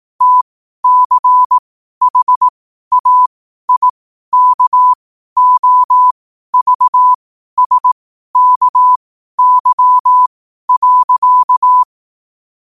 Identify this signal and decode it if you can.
Morse code, heard as a keyed tone, about 18 wpm, spaced as for 12 wpm, 1000 Hz. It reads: TCHAIKOVSKY.